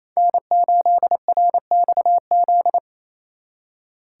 Morse code, heard as a keyed tone, about 34 wpm, 700 Hz. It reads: N8RXZ
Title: Speed Racing - USA 1x3 Call Signs 28wpm